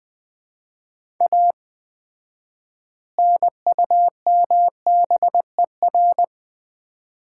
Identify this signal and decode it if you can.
Morse code, heard as a keyed tone, about 21 wpm, 700 Hz. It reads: A NUMBER
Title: 4x Word-Spacing - Sets of 3 Words - 20wpm